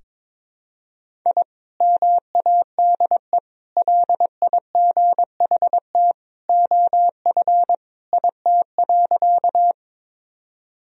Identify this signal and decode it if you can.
Morse code, heard as a keyed tone, about 22 wpm, 700 Hz. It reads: I MADE LIGHT OF IT.
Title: Sentences from Top 1000 Words - Review Part 1 - 22wpm